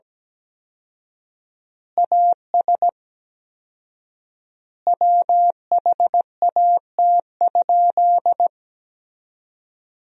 Morse code, heard as a keyed tone, about 17 wpm, 700 Hz. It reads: AS WHAT?
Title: 4x Word-Spacing - Sentences from Top 200 Words - 17wpm